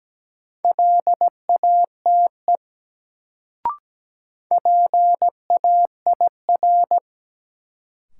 Morse code, heard as a keyed tone, about 17 wpm, 700 Hz. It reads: LATE PAIR
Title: Sets of 3 Words - New Word Drill - 17wpm